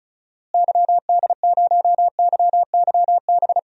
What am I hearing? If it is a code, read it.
Morse code, heard as a keyed tone, about 36 wpm, 700 Hz. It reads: YD0YYB
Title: DXCC-100 Call Signs - 6 Characters - Repeated 3x - 35wpm